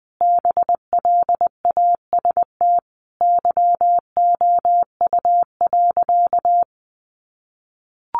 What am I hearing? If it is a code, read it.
Morse code, heard as a keyed tone, about 20 wpm, 700 Hz. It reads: BLAST YOU.
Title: A Sound Of Thunder 20wpm